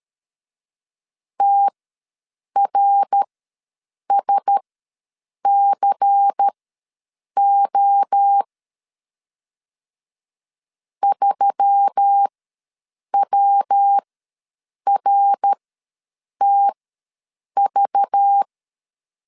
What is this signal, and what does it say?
Morse code, heard as a keyed tone, about 13 wpm, spaced as for 7 wpm, 800 Hz. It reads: TRSCO 3WRTV
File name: LEZIONI / 5 LBVC / 5 LBVC ws 12wpm 3159